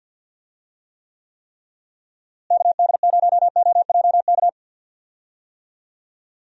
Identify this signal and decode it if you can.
Morse code, heard as a keyed tone, about 51 wpm, 700 Hz. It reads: KD0OJK
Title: USA Call Signs - 6 Characters - Repeated 3x - 50wpm